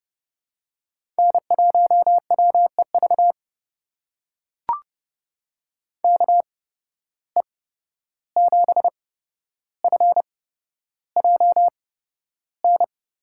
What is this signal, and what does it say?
Morse code, heard as a keyed tone, about 31 wpm, spaced as for 12 wpm, 700 Hz. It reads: N1WEV KE7FJN
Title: USA Call Signs - 8x Character Spacing - 30wpm